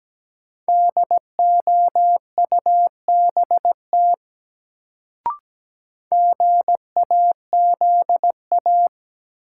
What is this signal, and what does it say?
Morse code, heard as a keyed tone, about 17 wpm, 700 Hz. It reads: DOUBT GAZA
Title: Sets of 5 Words - New Word Drill - 17wpm